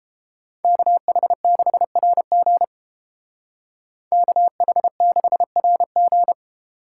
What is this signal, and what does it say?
Morse code, heard as a keyed tone, about 31 wpm, 700 Hz. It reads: KH6RG KH6RG